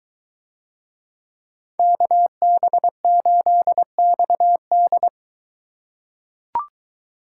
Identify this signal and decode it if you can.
Morse code, heard as a keyed tone, about 21 wpm, 700 Hz. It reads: KB8XD